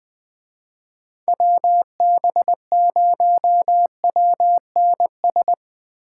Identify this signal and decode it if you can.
Morse code, heard as a keyed tone, about 21 wpm, 700 Hz. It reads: WB0WNS